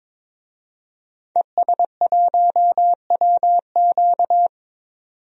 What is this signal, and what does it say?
Morse code, heard as a keyed tone, about 22 wpm, 700 Hz. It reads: ES1WQ
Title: DXCC-100 Call Signs - 5 Characters - 22wpm